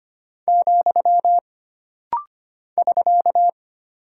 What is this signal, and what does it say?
Morse code, heard as a keyed tone, about 25 wpm, 700 Hz. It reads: , <SK>
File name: Single Character (Letter:Number:Punctuation:Prosign) - Rapid-Fire - 25wpm